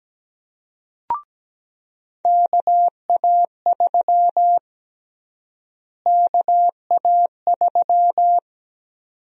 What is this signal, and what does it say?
Morse code, heard as a keyed tone, about 17 wpm, 700 Hz. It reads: KA3 KA3
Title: USA Call Sign Prefixes - Repeated 2x - 17wpm